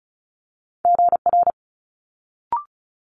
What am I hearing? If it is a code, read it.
Morse code, heard as a keyed tone, about 36 wpm, 700 Hz. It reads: GR
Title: Multi-letter Phonemes Quick Fire 35wpm